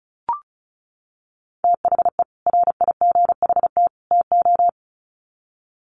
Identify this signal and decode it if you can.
Morse code, heard as a keyed tone, about 34 wpm, 700 Hz. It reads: THE RIGHT TO
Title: Sets of 3 Words 35wpm